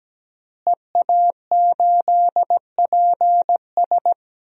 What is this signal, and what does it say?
Morse code, heard as a keyed tone, about 17 wpm, 700 Hz. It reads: EA8PS
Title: DXCC-100 Call Signs - 5 Characters - Repeated 3x - 17wpm